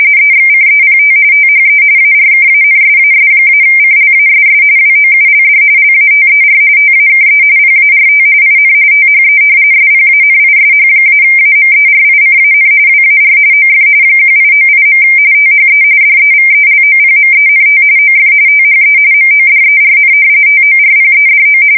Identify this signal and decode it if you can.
RTTY, about 45.45 baud. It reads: EXCEPT THE ARKANSAS-CLASS WOULD LATER TAKE PART IN THE SPANISH-AMERICAN WAR OF 1898. THE DEVELOPMENT OF THE MODERN TORPEDO
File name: fsk170 / usnavy3